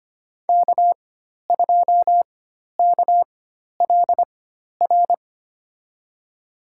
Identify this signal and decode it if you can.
Morse code, heard as a keyed tone, about 25 wpm, spaced as for 12 wpm, 700 Hz. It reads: K2KLR
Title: USA Call Signs - 5 Characters - 4x Character Spacing - 25wpm